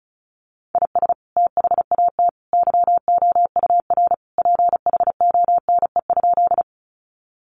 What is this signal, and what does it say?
Morse code, heard as a keyed tone, about 34 wpm, 700 Hz. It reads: IS THAT YOUR PHONE?
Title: Sentences from Top 1100 Words 35wpm